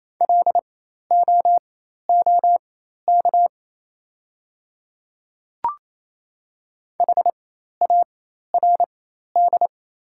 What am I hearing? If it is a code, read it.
Morse code, heard as a keyed tone, about 28 wpm, spaced as for 13 wpm, 700 Hz. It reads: LOOK HARD
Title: 4x Character Spacing - Top 300 Words - 28wpm